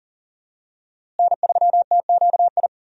Morse code, heard as a keyed tone, about 40 wpm, 700 Hz. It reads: N3TQI
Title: USA Call Signs - 5 Characters - Repeated 2x - 40wpm